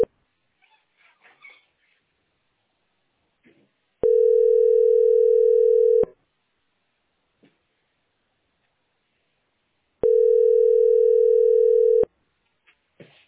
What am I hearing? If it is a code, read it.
Ringback tone (North America), 440 + 480 Hz.